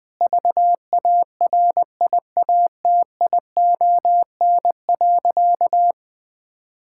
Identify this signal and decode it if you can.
Morse code, heard as a keyed tone, about 20 wpm, 700 Hz. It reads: VARIATION.
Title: Sentences from Top 1600 Words 20wpm